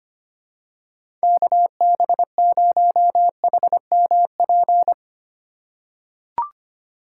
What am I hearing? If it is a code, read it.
Morse code, heard as a keyed tone, about 25 wpm, 700 Hz. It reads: KB0HMP
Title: USA Call Signs - 6 Characters - 4x Character Spacing - 25wpm